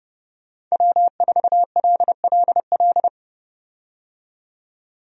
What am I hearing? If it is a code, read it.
Morse code, heard as a keyed tone, about 30 wpm, 700 Hz. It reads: W4LLL
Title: USA Call Signs - 5 Characters - 30wpm